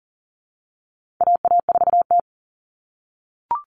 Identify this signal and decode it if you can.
Morse code, heard as a keyed tone, about 40 wpm, 700 Hz. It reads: AA4T
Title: All Cut Numbers - 4-Digit 40wpm